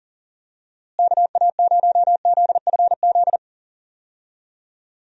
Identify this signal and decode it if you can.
Morse code, heard as a keyed tone, about 40 wpm, 700 Hz. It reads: KA0ZFZ
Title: USA Call Signs - 6 Characters - Repeated 2x - 40wpm